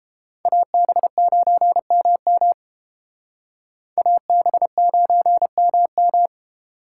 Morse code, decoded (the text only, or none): AB9MM AB9MM